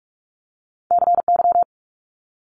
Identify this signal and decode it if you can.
Morse code, heard as a keyed tone, about 46 wpm, 700 Hz. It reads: CY
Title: Multi-letter Phonemes Quick Fire 45wpm